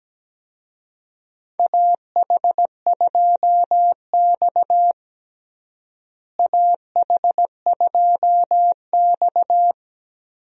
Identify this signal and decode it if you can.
Morse code, heard as a keyed tone, about 17 wpm, 700 Hz. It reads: AH2X AH2X
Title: USA Call Signs - 4 Characters - Repeated 2x - 17wpm